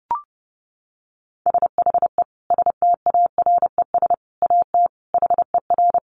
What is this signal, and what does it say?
Morse code, heard as a keyed tone, about 28 wpm, 700 Hz. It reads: SHE STARES AT HER
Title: Easy Phrases 30wpm